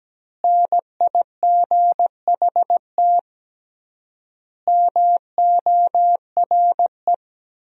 Morse code, heard as a keyed tone, about 17 wpm, 700 Hz. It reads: NIGHT MORE
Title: 3x Word-Spacing - Sentences from Top 300 Words - 17wpm